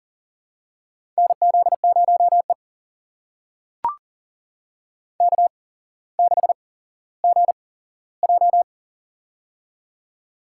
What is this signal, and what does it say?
Morse code, heard as a keyed tone, about 40 wpm, spaced as for 17 wpm, 700 Hz. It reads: NZ0E K6GJ
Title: USA Call Signs - 4 Characters - 8x Character Spacing - 40wpm